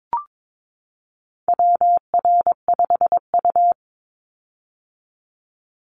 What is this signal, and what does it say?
Morse code, heard as a keyed tone, about 22 wpm, 700 Hz. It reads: WR5U
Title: Call Signs - 22wpm